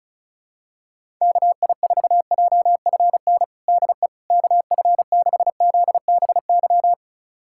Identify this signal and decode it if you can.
Morse code, heard as a keyed tone, about 36 wpm, 700 Hz. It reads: KI4JFN DE KF6ZBY
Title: USA Callsign DE Callsign - 6 Characters - 35wpm